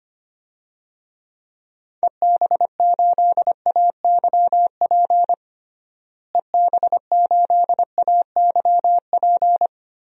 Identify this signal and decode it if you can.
Morse code, heard as a keyed tone, about 25 wpm, 700 Hz. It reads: EB8AYP EB8AYP
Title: DXCC-100 Call Signs - 6 Characters - Repeated 3x - 25wpm